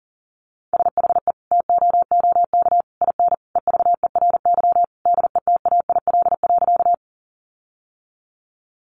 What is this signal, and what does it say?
Morse code, heard as a keyed tone, about 40 wpm, 700 Hz. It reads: SHE TOOK IN EVERY DETAIL.
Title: Sentences from Top 1500 Words 40wpm